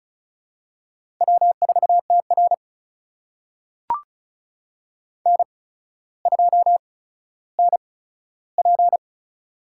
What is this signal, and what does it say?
Morse code, heard as a keyed tone, about 34 wpm, spaced as for 15 wpm, 700 Hz. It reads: W4TR N2NP
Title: USA Call Signs - 4 Characters - 8x Character Spacing - 35wpm